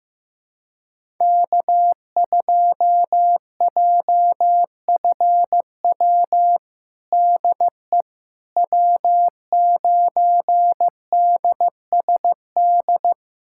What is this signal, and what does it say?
Morse code, heard as a keyed tone, about 16 wpm, 700 Hz. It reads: K2JFW DE W9DSD
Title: USA Callsign DE Callsign - 5 Characters - 15wpm